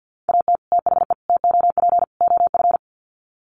Morse code, heard as a keyed tone, about 50 wpm, 700 Hz. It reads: AT THE TOP OF
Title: Sets of 4 Words 50wpm